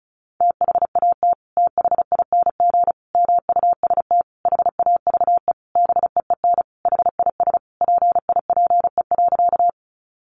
Morse code, heard as a keyed tone, about 34 wpm, 700 Hz. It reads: THAT THING MUST HAVE BEEN HIS PIPE.
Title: Sentences from Top 1500 Words 35wpm